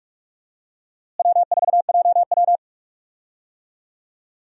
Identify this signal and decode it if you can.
Morse code, heard as a keyed tone, about 41 wpm, 700 Hz. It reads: W4JW
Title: USA Call Signs - 4 Characters - Repeated 3x - 45wpm